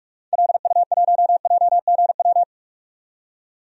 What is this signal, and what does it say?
Morse code, heard as a keyed tone, about 46 wpm, 700 Hz. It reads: LU1JGW
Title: DXCC-100 Call Signs - 6 Characters - 45wpm